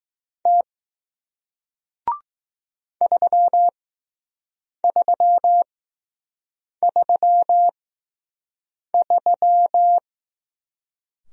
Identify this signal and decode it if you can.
Morse code, heard as a keyed tone, about 21 wpm, spaced as for 6 wpm, 700 Hz. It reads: T 3333